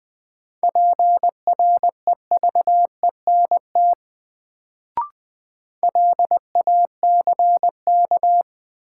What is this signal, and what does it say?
Morse code, heard as a keyed tone, about 20 wpm, 700 Hz. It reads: PREVENT LACK
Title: Sentences from Top 1100 Words - New Word Drill Alt - 20wpm